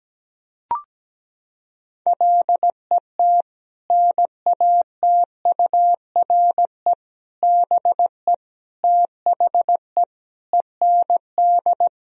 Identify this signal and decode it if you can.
Morse code, heard as a keyed tone, about 17 wpm, 700 Hz. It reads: LET NATURE BE THE END
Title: Sentences from Top 1000 Words - Review Part 3 - 17wpm